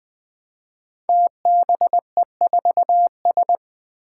Morse code, heard as a keyed tone, about 20 wpm, 700 Hz. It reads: TBE4S
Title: Five Challenging Letter-Number Combinations 20wpm